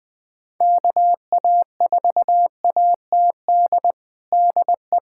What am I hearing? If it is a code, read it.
Morse code, heard as a keyed tone, about 20 wpm, 700 Hz. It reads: KA4ATD DE
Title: USA Callsign DE Callsign - 6 Characters - 20wpm